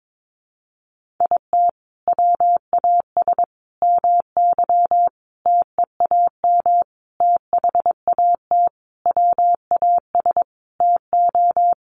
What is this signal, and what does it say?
Morse code, heard as a keyed tone, about 22 wpm, 700 Hz. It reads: IT WAS MY TEAM THAT WAS TO
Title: Sentences from Top 1500 Words 22wpm